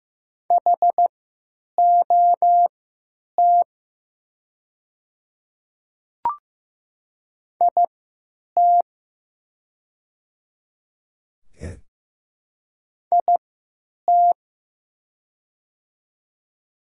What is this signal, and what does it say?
Morse code, heard as a keyed tone, about 15 wpm, spaced as for 9 wpm, 700 Hz. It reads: HOT IT IT
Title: 3x Character Spacing - Top 500 Words - 15wpm